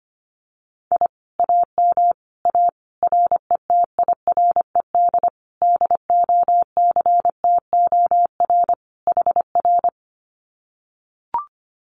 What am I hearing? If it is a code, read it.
Morse code, heard as a keyed tone, about 25 wpm, 700 Hz. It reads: I AM A RETIRED DOCTOR HR
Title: Common QSO Phrases Encore 25wpm